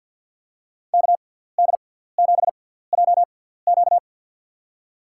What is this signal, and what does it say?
Morse code, heard as a keyed tone, about 53 wpm, spaced as for 17 wpm, 700 Hz. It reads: KD7JQ
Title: USA Call Signs - 5 Characters - 6x Character Spacing - 50wpm